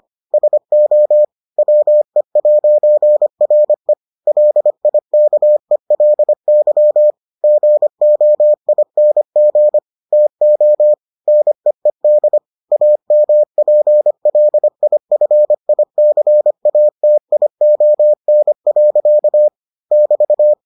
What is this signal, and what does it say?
Morse code, heard as a keyed tone, about 25 wpm, 600 Hz. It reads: SO WE'RE LIKELY GOING TO NEED AMPLIFICATION. <BT>